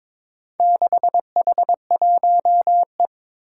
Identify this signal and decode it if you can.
Morse code, heard as a keyed tone, about 22 wpm, 700 Hz. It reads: 6H1E
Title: Four Challenging Letter-Number Combinations 22wpm